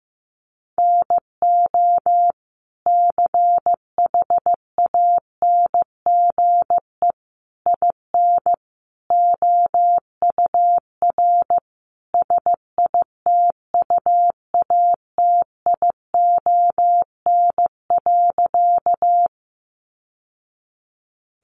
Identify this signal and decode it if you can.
Morse code, heard as a keyed tone, about 15 wpm, 700 Hz. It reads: NO CHANGE IN OUR SITUATION.
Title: Sentences from Top 1200 Words 15wpm